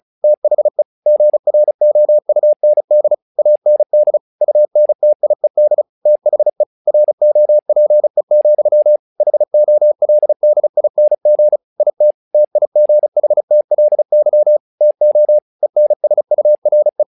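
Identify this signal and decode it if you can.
Morse code, heard as a keyed tone, about 36 wpm, 600 Hz. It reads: THE GROUND AND UNTIED THE ROPE, HOLDING IT TIGHTLY TO ENSURE
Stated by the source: Morse practice text "Tower Trouble" sent at 35 WPM, from the SKCC - Straight Key Century Club learning center audio files.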